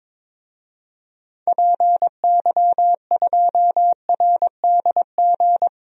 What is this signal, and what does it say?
Morse code, heard as a keyed tone, about 22 wpm, 700 Hz. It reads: PY2RDG
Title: DXCC-100 Call Signs - 6 Characters - Repeated 3x - 22wpm